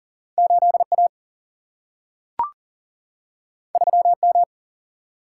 Morse code, heard as a keyed tone, about 40 wpm, 700 Hz. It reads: 8A 3M
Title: Single-Letter Single-Number 40wpm